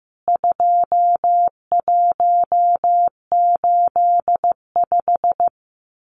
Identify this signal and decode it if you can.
Morse code, heard as a keyed tone, about 15 wpm, 700 Hz. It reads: 2185
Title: CWT Exchange 15wpm